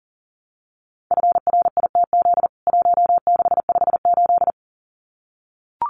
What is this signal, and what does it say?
Morse code, heard as a keyed tone, about 40 wpm, 700 Hz. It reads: FRITZ 1658
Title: CWT Exchange 40wpm